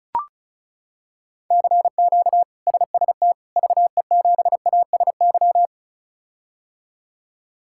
Morse code, heard as a keyed tone, about 34 wpm, 700 Hz. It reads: CQ SST VE7ASY
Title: K1USN SST QSO Practice - 35wpm